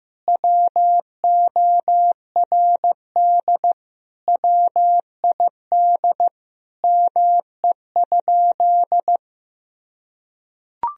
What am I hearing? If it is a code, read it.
Morse code, heard as a keyed tone, about 16 wpm, 700 Hz. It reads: WORD WID ME?
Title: Sentences from Top 300 Words and Abbreviations 15wpm